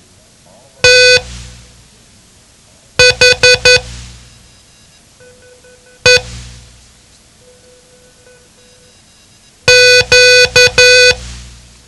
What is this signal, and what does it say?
Morse code, heard as a keyed tone, about 11 wpm, spaced as for 4 wpm, 500 Hz. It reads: THE Q
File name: ham radio / CodeQuick / Code Quick 07